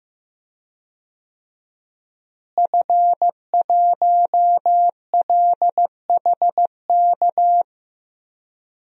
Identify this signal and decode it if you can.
Morse code, heard as a keyed tone, about 15 wpm, 700 Hz. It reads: F1LHK
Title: DXCC-100 Call Signs - 5 Characters - Repeated 2x - 15wpm